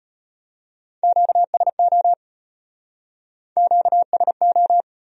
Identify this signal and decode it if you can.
Morse code, heard as a keyed tone, about 34 wpm, 700 Hz. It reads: QSO QSO